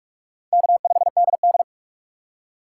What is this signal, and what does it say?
Morse code, heard as a keyed tone, about 44 wpm, 700 Hz. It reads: K5DD